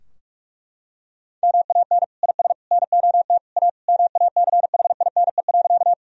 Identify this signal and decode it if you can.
Morse code, heard as a keyed tone, about 46 wpm, 700 Hz. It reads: MAN IS NOT A MACHINE.
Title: Sentences from Top 1000 Words - Review Part 2 - 45wpm